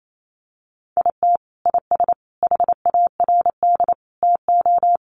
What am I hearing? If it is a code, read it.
Morse code, heard as a keyed tone, about 28 wpm, 700 Hz. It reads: IT IS HARD TO
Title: Sets of 5 Words Encore2 28wpm